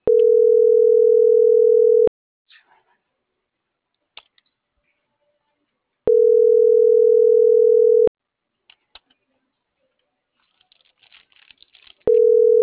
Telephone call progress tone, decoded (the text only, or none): ringback tone